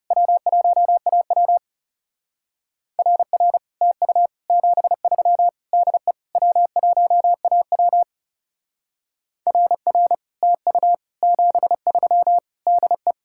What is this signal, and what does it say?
Morse code, heard as a keyed tone, about 38 wpm, 700 Hz. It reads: W1AW RR TU 73 DE W1AW RR TU 73 DE